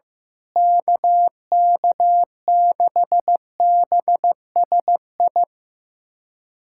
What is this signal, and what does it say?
Morse code, heard as a keyed tone, about 15 wpm, 700 Hz. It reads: KK6BSI
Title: USA Callsign DE Callsign - 6 Characters - 15wpm